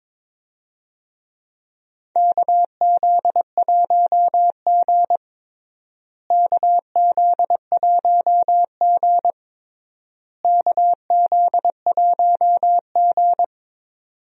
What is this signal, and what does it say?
Morse code, heard as a keyed tone, about 21 wpm, 700 Hz. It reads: KZ1G KZ1G KZ1G
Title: USA Call Signs - 4 Characters - Repeated 3x - 22wpm